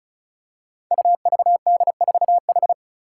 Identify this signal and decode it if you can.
Morse code, heard as a keyed tone, about 36 wpm, 700 Hz. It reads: UVD4H
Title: Five Challenging Letter-Number Combinations 35wpm